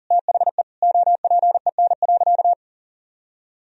Morse code, heard as a keyed tone, about 40 wpm, 700 Hz. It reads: THE OPEN.
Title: Sentences from Top 1000 Words - Review Part 1 - 40wpm